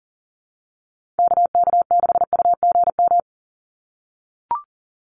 Morse code, heard as a keyed tone, about 40 wpm, 700 Hz. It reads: KK6UGM
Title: USA Call Signs 40wpm